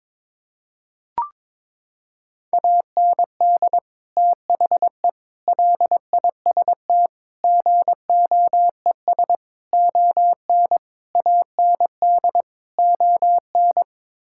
Morse code, heard as a keyed tone, about 22 wpm, 700 Hz. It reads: AND THE LIST GOES ON AND ON
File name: practice / 22wpm / Sets of 7 Words - 22wpm